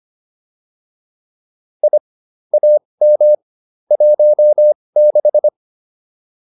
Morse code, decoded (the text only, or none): I AM 16